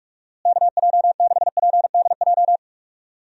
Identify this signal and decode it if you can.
Morse code, heard as a keyed tone, about 41 wpm, 700 Hz. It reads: KJ6PDJ
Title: USA Call Signs - 6 Characters - Repeated 2x - 45wpm